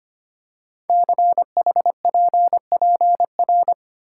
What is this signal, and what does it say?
Morse code, heard as a keyed tone, about 25 wpm, 700 Hz. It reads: CHPPR
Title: Code Groups - 5 Letters - 25wpm